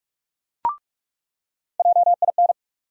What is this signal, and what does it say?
Morse code, heard as a keyed tone, about 45 wpm, 700 Hz. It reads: JIN
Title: USA Call Sign Postfixes - 3 Letters 45wpm